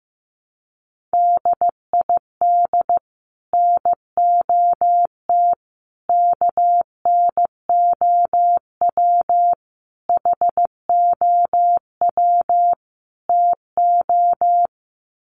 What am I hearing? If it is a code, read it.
Morse code, heard as a keyed tone, about 15 wpm, 700 Hz. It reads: DID NOT KNOW HOW TO
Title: Sets of 5 Words 15wpm